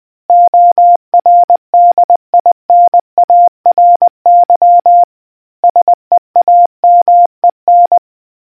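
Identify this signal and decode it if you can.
Morse code, heard as a keyed tone, about 20 wpm, 700 Hz. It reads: ORDINARY SEAMEN
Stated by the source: Morse quiz question from the RNZN Communicators Association.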